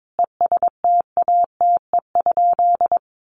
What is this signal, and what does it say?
Morse code, heard as a keyed tone, about 22 wpm, 700 Hz. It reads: ESTATE?
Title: Sentences from Top 1200 Words 22wpm